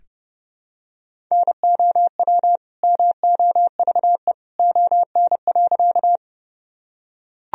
Morse code, heard as a keyed tone, about 31 wpm, 700 Hz. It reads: NOW MOVE ON.
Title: Sentences from Top 1000 Words - Review Part 1 - 30wpm